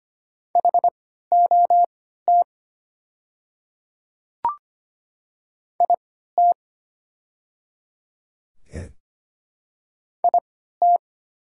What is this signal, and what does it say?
Morse code, heard as a keyed tone, about 25 wpm, spaced as for 14 wpm, 700 Hz. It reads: HOT IT IT
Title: 3x Character Spacing - Top 500 Words - 25wpm